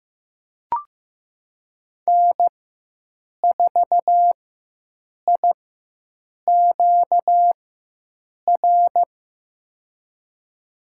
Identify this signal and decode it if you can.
Morse code, heard as a keyed tone, about 15 wpm, spaced as for 7 wpm, 700 Hz. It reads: N4IQR
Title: USA Call Signs - 5 Characters - 4x Character Spacing - 15wpm